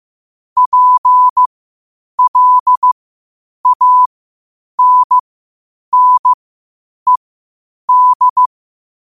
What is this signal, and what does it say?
Morse code, heard as a keyed tone, about 15 wpm, spaced as for 8 wpm, 1000 Hz. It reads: PLANNED
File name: qotdfiles / 8wpm